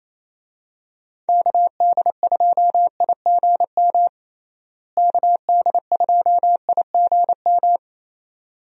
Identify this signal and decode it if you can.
Morse code, heard as a keyed tone, about 28 wpm, 700 Hz. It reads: KD2IGM KD2IGM
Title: USA Call Signs - 6 Characters - Repeated 2x - 28wpm